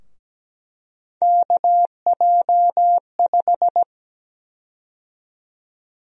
Morse code, heard as a keyed tone, about 17 wpm, 700 Hz. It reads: KJ5
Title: USA Call Sign Prefixes - 17wpm